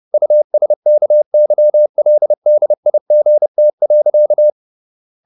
Morse code, heard as a keyed tone, about 30 wpm, 600 Hz. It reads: USKYLDIGT.
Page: Ferie 30 wpm